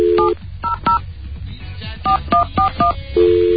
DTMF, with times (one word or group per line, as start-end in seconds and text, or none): 0.2-1.0 700
2.1-2.9 4141